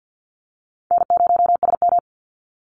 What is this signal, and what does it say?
Morse code, heard as a keyed tone, about 50 wpm, 700 Hz. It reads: N0SM
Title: Call Signs - 50wpm